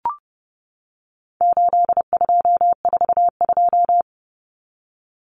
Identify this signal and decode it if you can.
Morse code, heard as a keyed tone, about 30 wpm, 700 Hz. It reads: 8242
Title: Common Cut Numbers - 4-Digit 30wpm